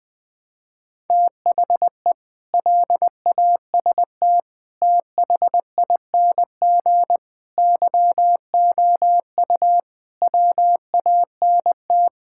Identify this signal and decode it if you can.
Morse code, heard as a keyed tone, about 21 wpm, 700 Hz. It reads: THE LAST THING YOU WANT
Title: Sets of 8 Words - 20wpm